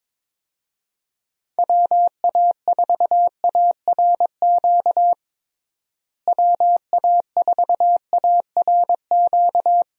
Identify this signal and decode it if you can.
Morse code, heard as a keyed tone, about 21 wpm, 700 Hz. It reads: WA4ARQ WA4ARQ
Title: USA Call Signs - 6 Characters - Repeated 3x - 22wpm